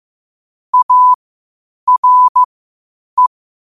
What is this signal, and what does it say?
Morse code, heard as a keyed tone, about 15 wpm, spaced as for 8 wpm, 1000 Hz. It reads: ARE
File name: qotdfiles / 8wpm